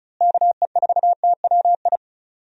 Morse code, heard as a keyed tone, about 34 wpm, 700 Hz. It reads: KE4TWI
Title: USA Call Signs - 6 Characters - Repeated 3x - 35wpm